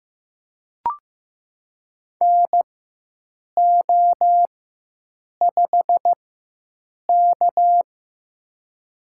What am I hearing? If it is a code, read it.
Morse code, heard as a keyed tone, about 16 wpm, spaced as for 7 wpm, 700 Hz. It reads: NO5K
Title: USA Call Signs - 4 Characters - 4x Character Spacing - 15wpm